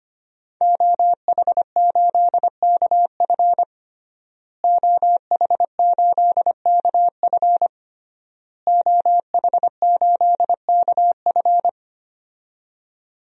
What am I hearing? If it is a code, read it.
Morse code, heard as a keyed tone, about 25 wpm, 700 Hz. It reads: OH8KF OH8KF OH8KF
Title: DXCC-100 Call Signs - 5 Characters - Repeated 3x - 25wpm